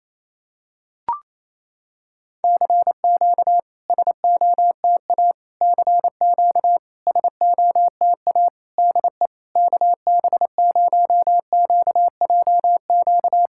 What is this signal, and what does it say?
Morse code, heard as a keyed tone, about 28 wpm, 700 Hz. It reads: CQ SOTA CQ SOTA DE KB0QJQ
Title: SOTA QSO Practice 28wpm